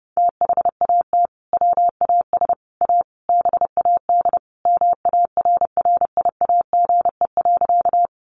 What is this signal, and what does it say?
Morse code, heard as a keyed tone, about 30 wpm, 700 Hz. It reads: THAT WAS A BAD MARRIAGE.
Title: Sentences from Top 800 Words 30wpm